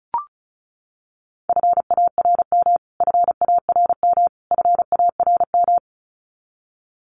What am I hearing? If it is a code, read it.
Morse code, heard as a keyed tone, about 34 wpm, 700 Hz. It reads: FARM FARM FARM
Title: Top 500 Words Repeated 35wpm